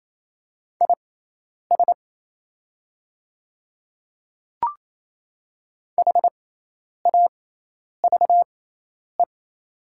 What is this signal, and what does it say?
Morse code, heard as a keyed tone, about 28 wpm, spaced as for 10 wpm, 700 Hz. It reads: IS HAVE
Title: 6x Character Spacing - Top 100 Words - 28wpm